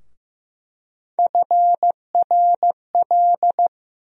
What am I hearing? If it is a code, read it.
Morse code, heard as a keyed tone, about 15 wpm, 700 Hz. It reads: FRL
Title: USA Call Sign Postfixes - 3 Letters 15wpm